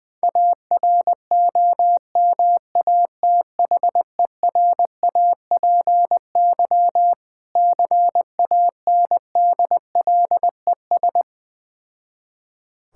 Morse code, heard as a keyed tone, about 21 wpm, 690 Hz. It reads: AROMATHERAPY CANDLES
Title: Difficult Words with Hints - 20wpm